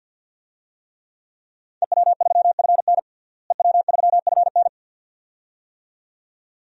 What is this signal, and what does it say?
Morse code, heard as a keyed tone, about 51 wpm, 700 Hz. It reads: EW3FN EW3FN
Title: DXCC-100 Call Signs - 5 Characters - Repeated 2x - 50wpm